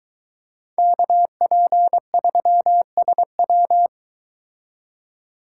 Morse code, heard as a keyed tone, about 22 wpm, 700 Hz. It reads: KP3SW